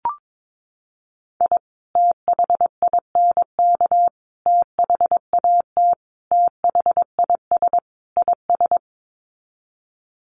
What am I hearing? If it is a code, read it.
Morse code, heard as a keyed tone, about 22 wpm, 700 Hz. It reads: I THINK THAT THIS IS
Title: Sets of 5 Words Encore2 22wpm